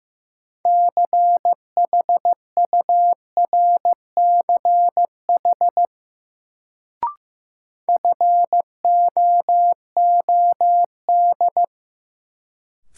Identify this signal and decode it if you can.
Morse code, heard as a keyed tone, about 15 wpm, 700 Hz. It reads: CHURCH FOOD
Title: Sets of 5 Words - New Word Drill - 15wpm